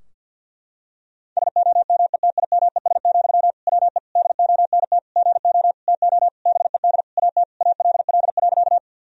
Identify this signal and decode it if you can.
Morse code, heard as a keyed tone, about 52 wpm, 700 Hz. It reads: SOMETIMES, WE DONT GO TO BED AT ALL.
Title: Sentences from Top 1000 Words - Review Part 3 - 50wpm